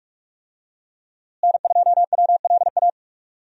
Morse code, heard as a keyed tone, about 41 wpm, 700 Hz. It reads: N2WLA
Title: USA Call Signs - 5 Characters - Repeated 2x - 45wpm